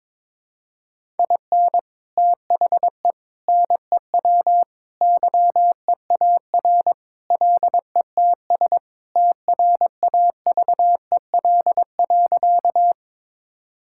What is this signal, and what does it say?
Morse code, heard as a keyed tone, about 22 wpm, 700 Hz. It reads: IN THE NEW YEAR LETS TRAVEL.
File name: practice / 22wpm / Sentences with 3-Word Phrases - 22wpm